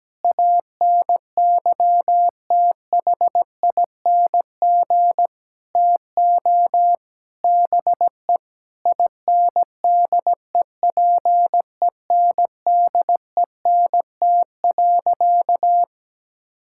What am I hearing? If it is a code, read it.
Morse code, heard as a keyed tone, about 17 wpm, 700 Hz. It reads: ANYTHING TO BE INDEPENDENT.
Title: Sentences from Top 1000 Words - Review Part 6 - 17wpm